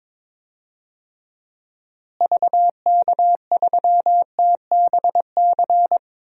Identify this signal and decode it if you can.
Morse code, heard as a keyed tone, about 22 wpm, 700 Hz. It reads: VK3TBC
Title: DXCC-100 Call Signs - 6 Characters - Repeated 3x - 22wpm